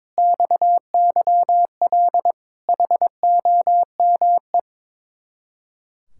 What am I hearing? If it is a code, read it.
Morse code, heard as a keyed tone, about 22 wpm, 700 Hz. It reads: XYL HOME
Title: Speed-Racing Common QSO Phrases - 22wpm